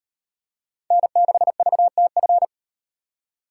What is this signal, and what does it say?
Morse code, heard as a keyed tone, about 37 wpm, 700 Hz. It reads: N6VTF